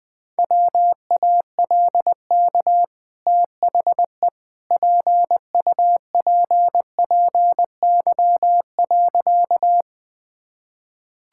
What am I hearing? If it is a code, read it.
Morse code, heard as a keyed tone, about 20 wpm, 700 Hz. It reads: WALK THE PUPPY.
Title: Easy Sentences with Hints - 20wpm